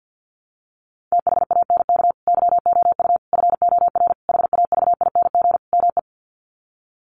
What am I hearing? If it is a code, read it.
Morse code, heard as a keyed tone, about 50 wpm, 700 Hz. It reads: THANK YOU FOR HAVING ME
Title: Sets of 5 Words Encore 50wpm